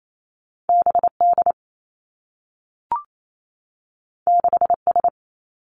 Morse code, heard as a keyed tone, about 28 wpm, 700 Hz. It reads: B D 6 S
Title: Two Challenging Letter-Number Combinations 28wpm